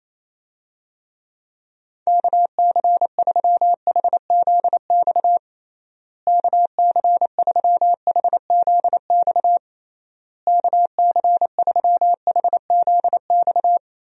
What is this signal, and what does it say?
Morse code, heard as a keyed tone, about 28 wpm, 700 Hz. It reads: KC3HZX KC3HZX KC3HZX
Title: USA Call Signs - 6 Characters - Repeated 3x - 28wpm